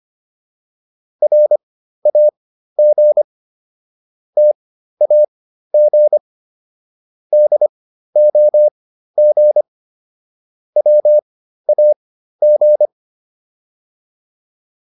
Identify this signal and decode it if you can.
Morse code, heard as a keyed tone, about 25 wpm, spaced as for 13 wpm, 600 Hz. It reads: RAG TAG DOG WAG